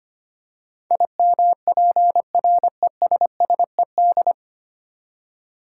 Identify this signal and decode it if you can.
Morse code, heard as a keyed tone, about 25 wpm, 700 Hz. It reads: IMPRESSED
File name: Nine Letter Words - 25wpm